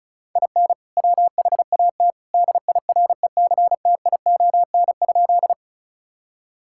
Morse code, heard as a keyed tone, about 36 wpm, 700 Hz. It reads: IN WHAT DIRECTION?
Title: Sentences from Top 1000 Words - Review Part 5 - 35wpm